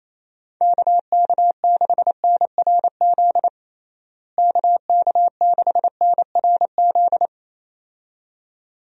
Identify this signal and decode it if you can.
Morse code, heard as a keyed tone, about 27 wpm, 700 Hz. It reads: KK6NRZ KK6NRZ
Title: USA Call Signs - 6 Characters - Repeated 2x - 28wpm